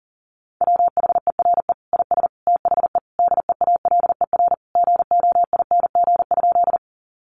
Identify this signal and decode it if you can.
Morse code, heard as a keyed tone, about 40 wpm, 700 Hz. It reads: WHERE IS THE DEALER GOING?
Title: Sentences from Top 1100 Words 40wpm